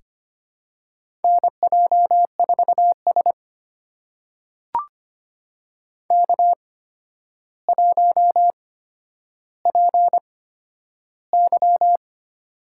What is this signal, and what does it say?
Morse code, heard as a keyed tone, about 25 wpm, spaced as for 11 wpm, 700 Hz. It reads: NJ4S K1PY
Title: USA Call Signs - 4 Characters - 8x Character Spacing - 25wpm